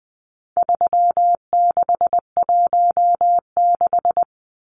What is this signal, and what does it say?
Morse code, heard as a keyed tone, about 20 wpm, 700 Hz. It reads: 3616
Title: Numbers - 4-Digit 20wpm